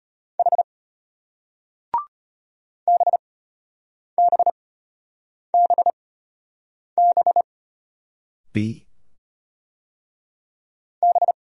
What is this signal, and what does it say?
Morse code, heard as a keyed tone, about 33 wpm, spaced as for 8 wpm, 700 Hz. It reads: H BBBB B